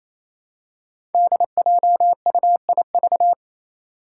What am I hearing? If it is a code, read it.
Morse code, heard as a keyed tone, about 28 wpm, 700 Hz. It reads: DJUIV
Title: Five Challenging Letter-Number Combinations 28wpm